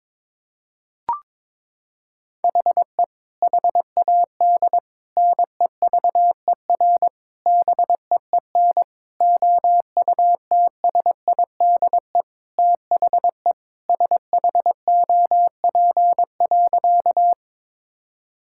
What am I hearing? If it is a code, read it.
Morse code, heard as a keyed tone, about 22 wpm, 700 Hz. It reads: HE HAD NEVER BEEN OUTSIDE THE SHOP.
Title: Sentences from Top 1600 Words 22wpm